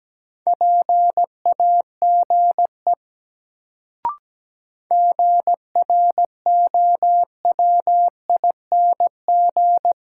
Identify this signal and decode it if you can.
Morse code, heard as a keyed tone, about 17 wpm, 700 Hz. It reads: PAGE GROWING
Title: Sets of 5 Words Encore - New Word Drill - 17wpm